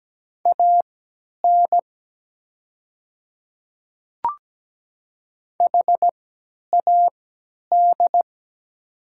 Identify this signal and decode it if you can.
Morse code, heard as a keyed tone, about 17 wpm, spaced as for 10 wpm, 700 Hz. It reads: AN HAD
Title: 3x Character Spacing - Top 100 Words - 17wpm